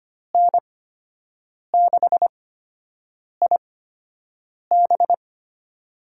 Morse code, heard as a keyed tone, about 25 wpm, spaced as for 7 wpm, 700 Hz. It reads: N6IB
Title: USA Call Signs - 8x Character Spacing - 25wpm